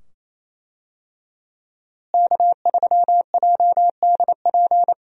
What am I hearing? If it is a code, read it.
Morse code, heard as a keyed tone, about 28 wpm, 700 Hz. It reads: K3JDP
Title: USA Call Signs - 5 Characters - Repeated 2x - 28wpm